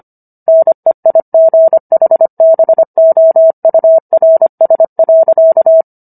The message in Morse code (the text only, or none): NEIGHBOURS.